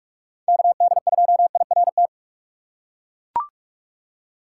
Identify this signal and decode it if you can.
Morse code, heard as a keyed tone, about 41 wpm, 700 Hz. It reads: KD2IRT
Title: USA Call Signs - 6 Characters - 3x Character Spacing - 45wpm